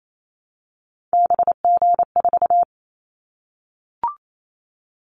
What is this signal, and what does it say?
Morse code, heard as a keyed tone, about 28 wpm, 700 Hz. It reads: BG4
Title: Callsign Prefixes for Top 100 DXCC - 28wpm